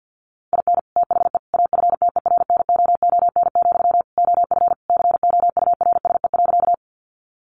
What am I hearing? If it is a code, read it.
Morse code, heard as a keyed tone, about 50 wpm, 700 Hz. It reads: IN THE AFTERNOON, OF COURSE.
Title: Sentences from Top 1500 Words 50wpm